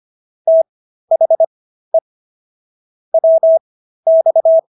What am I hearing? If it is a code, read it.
Morse code, heard as a keyed tone, about 25 wpm, spaced as for 13 wpm, 650 Hz. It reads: THE WX